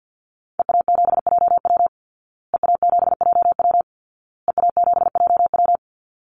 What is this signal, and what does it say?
Morse code, heard as a keyed tone, about 50 wpm, 700 Hz. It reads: EA7JW EA7JW EA7JW
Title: Call Signs Repeated 2x - 50wpm